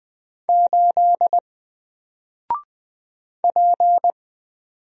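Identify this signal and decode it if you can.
Morse code, heard as a keyed tone, about 20 wpm, 700 Hz. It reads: 8 P
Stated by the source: Single letter and number Morse drill at 20 wpm, from N4LCD.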